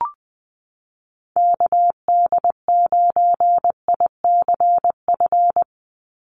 Morse code, heard as a keyed tone, about 20 wpm, 700 Hz. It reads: KD9ICF
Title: USA Call Signs 20wpm